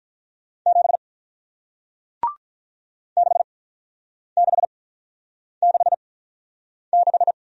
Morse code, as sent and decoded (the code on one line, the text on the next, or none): --... / -.... -.... -.... -....
7 6666